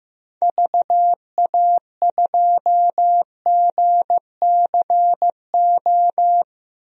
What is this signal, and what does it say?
Morse code, heard as a keyed tone, about 15 wpm, 700 Hz. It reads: VA2GCO
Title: DXCC-100 Call Signs - 6 Characters - Repeated 2x - 15wpm